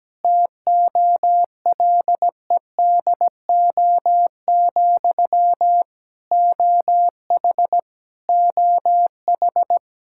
Morse code, heard as a keyed tone, about 17 wpm, 700 Hz. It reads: TOLEDO, OH OH
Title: Practice QSO - Exchange 1 - Beginning - 17wpm